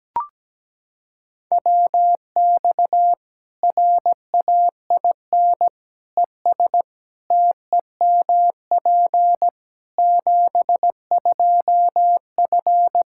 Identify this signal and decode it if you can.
Morse code, heard as a keyed tone, about 17 wpm, 700 Hz. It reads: WX RAIN ES TEMP 72F
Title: K1USN SST QSO Practice 17wpm